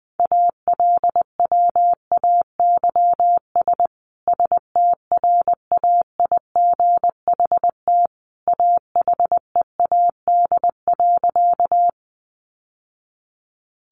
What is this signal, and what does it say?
Morse code, heard as a keyed tone, about 20 wpm, 700 Hz. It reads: ALWAYS STRAIGHT AHEAD.
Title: Sentences from Top 1100 Words 20wpm